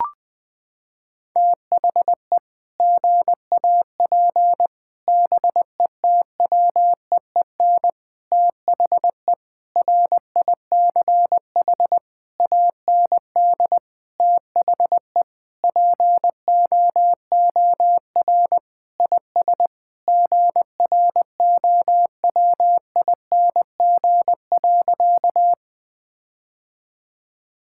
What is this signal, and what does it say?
Morse code, heard as a keyed tone, about 20 wpm, 700 Hz. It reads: THE GAP BETWEEN THE RICH AND THE POOR IS GROWING.
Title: Sentences with 8-Word Phrases - 20wpm